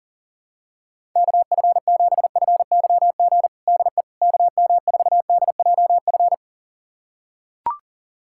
Morse code, heard as a keyed tone, about 40 wpm, 700 Hz. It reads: KF7FYG DE KM4DJF
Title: USA Callsign DE Callsign 40wpm